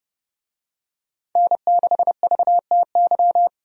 Morse code, heard as a keyed tone, about 32 wpm, 700 Hz. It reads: N6VTY
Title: USA Call Signs - 5 Characters - Repeated 3x - 30wpm